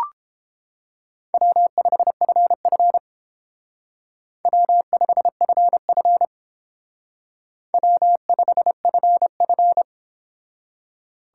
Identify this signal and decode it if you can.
Morse code, heard as a keyed tone, about 30 wpm, 700 Hz. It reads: W5FF W5FF W5FF